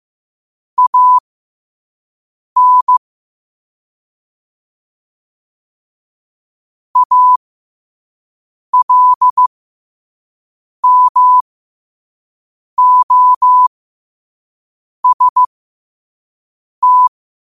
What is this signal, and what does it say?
Morse code, heard as a keyed tone, about 15 wpm, spaced as for 5 wpm, 1000 Hz. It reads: AN ALMOST
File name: qotdfiles / 5wpm